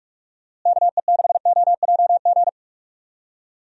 Morse code, heard as a keyed tone, about 46 wpm, 700 Hz. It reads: KE6OJG
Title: USA Call Signs - 6 Characters - Repeated 3x - 45wpm